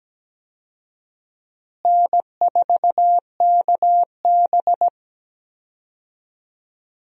Morse code, heard as a keyed tone, about 17 wpm, 700 Hz. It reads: N4KB
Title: USA Call Signs - 4 Characters - Repeated 2x - 17wpm